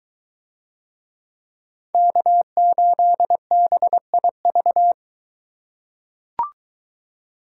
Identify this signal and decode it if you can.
Morse code, heard as a keyed tone, about 22 wpm, 700 Hz. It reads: K8BIV